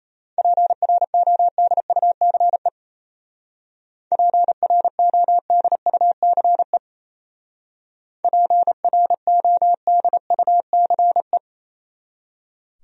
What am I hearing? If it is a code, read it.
Morse code, heard as a keyed tone, about 34 wpm, 700 Hz. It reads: PRODUCE PRODUCE PRODUCE
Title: Speed-Racing Top 500 Words 28wpm